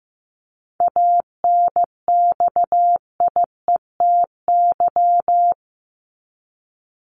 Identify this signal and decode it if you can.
Morse code, heard as a keyed tone, about 15 wpm, 700 Hz. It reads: ANXIETY
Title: Top 1501-1750 Words 15wpm